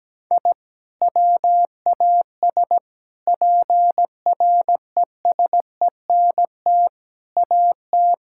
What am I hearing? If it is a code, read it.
Morse code, heard as a keyed tone, about 17 wpm, 700 Hz. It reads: I WAS PRESENT AT
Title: Sentences from Top 1000 Words - Review Part 5 - 17wpm